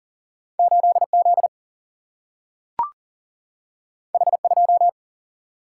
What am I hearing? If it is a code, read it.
Morse code, heard as a keyed tone, about 40 wpm, 700 Hz. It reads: 8 Z H 2
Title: Single-Letter Single-Number 40wpm